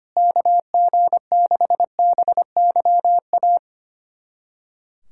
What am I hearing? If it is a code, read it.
Morse code, heard as a keyed tone, about 25 wpm, 700 Hz. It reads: KG6BYA
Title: USA Callsign DE Callsign 25wpm